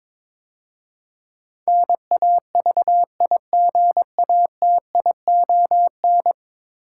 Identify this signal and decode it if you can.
Morse code, heard as a keyed tone, about 22 wpm, 700 Hz. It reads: NAVIGATION
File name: practice / 22wpm / 10 Letter Words - First 3 Letters Given - 22wpm